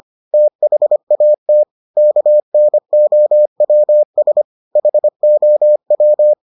Morse code, heard as a keyed tone, about 25 wpm, 600 Hz. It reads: THAT KNOWS HOW